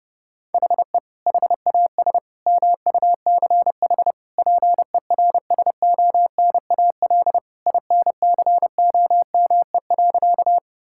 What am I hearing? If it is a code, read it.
Morse code, heard as a keyed tone, about 30 wpm, 700 Hz. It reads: HE HAS MUCH PERSONAL INCOME.
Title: Sentences from Top 1000 Words - Review Part 5 - 30wpm